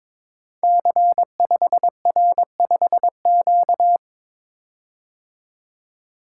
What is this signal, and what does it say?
Morse code, heard as a keyed tone, about 22 wpm, 700 Hz. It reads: C5R5Q
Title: Code Groups - 5 Letter:Numbers - 22wpm